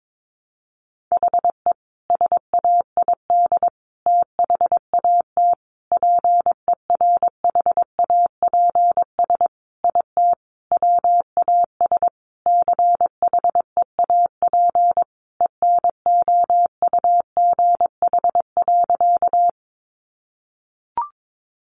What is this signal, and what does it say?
Morse code, heard as a keyed tone, about 22 wpm, 700 Hz. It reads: HE SAID THAT PERHAPS IT WAS CHEAP ENOUGH.
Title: Sentences from Top 1200 Words 22wpm